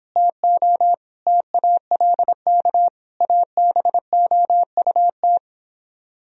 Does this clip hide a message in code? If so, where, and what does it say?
Yes: Morse code from 0.2 to 5.4 s, heard as a keyed tone, about 29 wpm, 700 Hz. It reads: TO TALK ABOUT